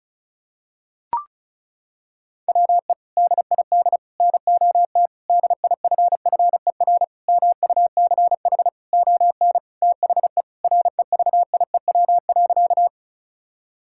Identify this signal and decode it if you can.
Morse code, heard as a keyed tone, about 34 wpm, 690 Hz. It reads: WE DID NOT DIFFER MUCH ON THE REVIEW.
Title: Sentences from Top 1000 Words - Review Part 6 - 35wpm